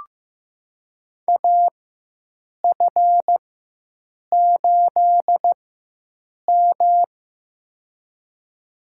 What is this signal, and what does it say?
Morse code, heard as a keyed tone, about 15 wpm, spaced as for 7 wpm, 700 Hz. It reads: AF8M
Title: USA Call Signs - 4 Characters - 4x Character Spacing - 15wpm